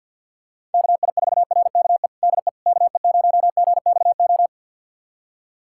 Morse code, heard as a keyed tone, about 50 wpm, 700 Hz. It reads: KI4RKE DE KE0GXO
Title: USA Callsign DE Callsign - 6 Characters - 50wpm